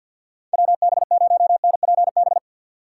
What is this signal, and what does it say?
Morse code, heard as a keyed tone, about 53 wpm, 700 Hz. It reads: WB0NPB
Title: USA Call Signs - 6x Character Spacing - 50wpm